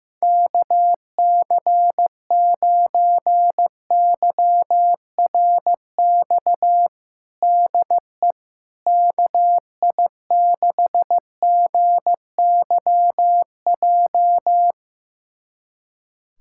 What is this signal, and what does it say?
Morse code, heard as a keyed tone, about 15 wpm, 700 Hz. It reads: KC9YRX DE KI6GYJ
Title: USA Callsign DE Callsign 15wpm